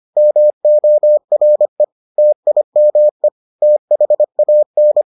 Morse code, heard as a keyed tone, about 25 wpm, 600 Hz. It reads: MORE TIME THAN